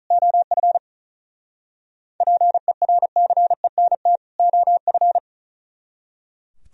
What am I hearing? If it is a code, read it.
Morse code, heard as a keyed tone, about 45 wpm, 700 Hz. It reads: OF PERCENT OF